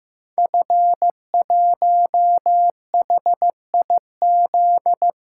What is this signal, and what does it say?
Morse code, heard as a keyed tone, about 15 wpm, 700 Hz. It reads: F1HIZ
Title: Code Groups - 5 Letter:Numbers - 15wpm